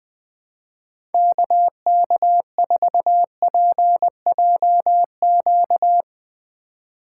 Morse code, heard as a keyed tone, about 20 wpm, 700 Hz. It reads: KK4PJQ
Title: Speed Racing - USA 2x3 Call Signs 20wpm